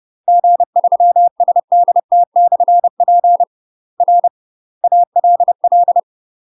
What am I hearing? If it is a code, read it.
Morse code, heard as a keyed tone, about 30 wpm, 700 Hz. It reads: G3SDT/P R ALL
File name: UK MASTER QSOs TRAINING/30WPM